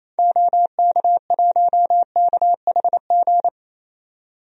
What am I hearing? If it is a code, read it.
Morse code, heard as a keyed tone, about 28 wpm, 700 Hz. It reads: OK1KHG
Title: DXCC-100 Call Signs - 6 Characters - 28wpm